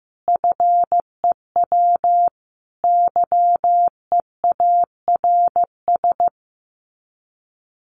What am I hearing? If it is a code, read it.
Morse code, heard as a keyed tone, about 15 wpm, 700 Hz. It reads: FEW YEARS
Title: Sets of 5 Words Encore 15wpm